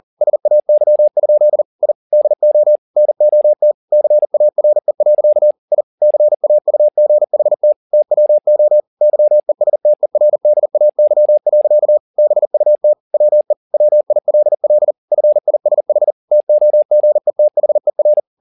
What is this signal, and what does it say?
Morse code, heard as a keyed tone, about 40 wpm, 600 Hz. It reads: SAY? I DO NOT CARE. I CAUGHT TWO YESTERDAY. BUT WE WILL FISH TOGETHER
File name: the old man and the sea 2 40wpm